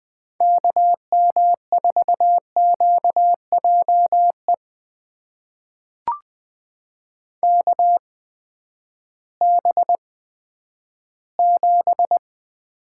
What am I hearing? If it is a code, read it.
Morse code, heard as a keyed tone, about 20 wpm, spaced as for 11 wpm, 700 Hz. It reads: KM4QJE KB7
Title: USA Call Signs - 6 Characters - 8x Character Spacing - 20wpm